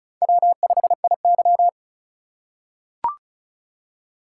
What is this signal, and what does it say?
Morse code, heard as a keyed tone, about 36 wpm, 700 Hz. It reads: W5IY
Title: USA Call Signs - 4 Characters - 6x Character Spacing - 35wpm